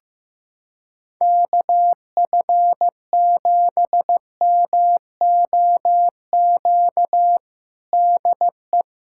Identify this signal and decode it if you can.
Morse code, heard as a keyed tone, about 14 wpm, 700 Hz. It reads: KF7MOQ DE
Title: USA Callsign DE Callsign - 6 Characters - 15wpm